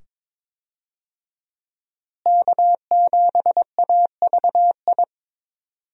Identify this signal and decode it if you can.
Morse code, heard as a keyed tone, about 22 wpm, 700 Hz. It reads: K7AVI
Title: USA Call Signs - 5 Characters - Repeated 2x - 22wpm